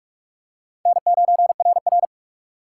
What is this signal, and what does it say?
Morse code, heard as a keyed tone, about 43 wpm, 700 Hz. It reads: N9RR